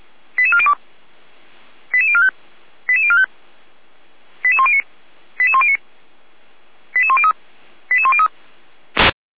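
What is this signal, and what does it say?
Five-tone selective call (ZVEI), heard as quick runs of five tones, tones of about 70 ms. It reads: 88402 88045 88045 88109 88109 88103 88103
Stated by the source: ZVEI five-tone selective call 88192, received on an SDR, from the feuerwehr-orlamuende fire brigade archive.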